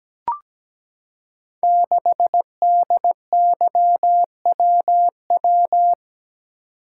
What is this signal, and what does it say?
Morse code, heard as a keyed tone, about 17 wpm, 700 Hz. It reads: 6DYWW
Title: Code Groups - 5 Letter:Numbers - 17wpm